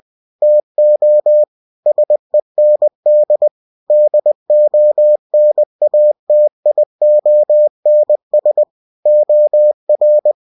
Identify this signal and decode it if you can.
Morse code, heard as a keyed tone, about 20 wpm, 600 Hz. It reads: TO SEND DONATIONS OR
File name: cw-books / winnie-the-pooh / winnie-20wpm-page-0121